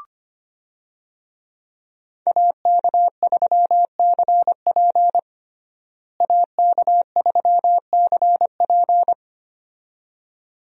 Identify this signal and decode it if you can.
Morse code, heard as a keyed tone, about 25 wpm, 700 Hz. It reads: AK3CP AK3CP
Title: USA Call Signs - 5 Characters - Repeated 2x - 25wpm